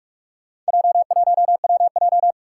Morse code, heard as a keyed tone, about 44 wpm, 700 Hz. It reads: J1WJ
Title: Four Challenging Letter-Number Combinations 45wpm